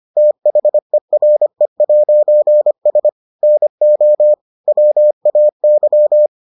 Morse code, heard as a keyed tone, about 25 wpm, 600 Hz. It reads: THERE'S NO WAY